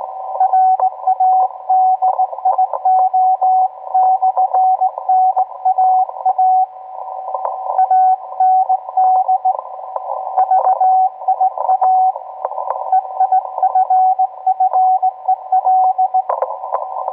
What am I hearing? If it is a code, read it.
Morse code, heard as a keyed tone, about 18 wpm, 750 Hz, with radio noise. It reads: RADIO/NAA AND VIA EIFFEL